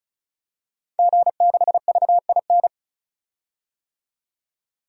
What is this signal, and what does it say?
Morse code, heard as a keyed tone, about 34 wpm, 700 Hz. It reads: G6VIN
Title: DXCC-100 Call Signs - 5 Characters - 35wpm